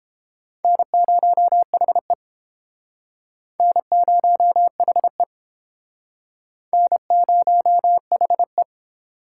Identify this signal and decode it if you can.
Morse code, heard as a keyed tone, about 29 wpm, 700 Hz. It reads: N0HE N0HE N0HE